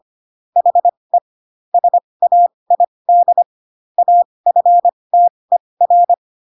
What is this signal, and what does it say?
Morse code, heard as a keyed tone, about 25 wpm, spaced as for 20 wpm, 700 Hz. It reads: HE SAID AFTER
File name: Pinocchio / Pinocchio ch7 15 25 ik1hge